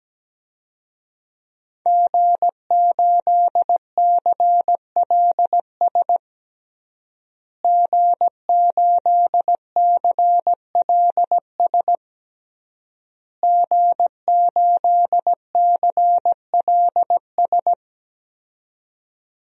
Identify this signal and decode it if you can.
Morse code, heard as a keyed tone, about 17 wpm, 700 Hz. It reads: G8CLS G8CLS G8CLS
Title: DXCC-100 Call Signs - 5 Characters - Repeated 3x - 17wpm